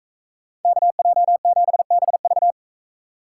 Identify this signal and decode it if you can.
Morse code, heard as a keyed tone, about 44 wpm, 700 Hz. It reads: KJ7BV